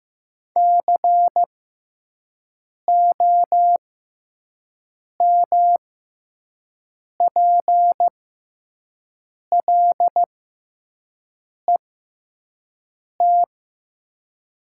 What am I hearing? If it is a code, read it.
Morse code, heard as a keyed tone, about 15 wpm, spaced as for 5 wpm, 700 Hz. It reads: COMPLET
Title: 6x Character Spacing - Top 500 Words - 15wpm